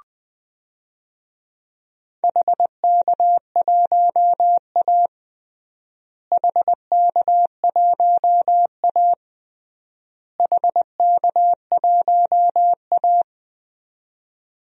Morse code, heard as a keyed tone, about 20 wpm, 700 Hz. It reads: HK1A HK1A HK1A
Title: DXCC-100 Call Signs - 4 Characters - Repeated 3x - 20wpm